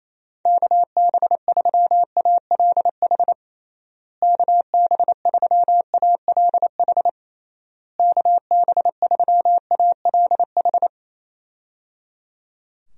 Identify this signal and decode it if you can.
Morse code, heard as a keyed tone, about 28 wpm, 700 Hz. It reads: KB3ALH KB3ALH KB3ALH
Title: USA Call Signs - 6 Characters - Repeated 3x - 28wpm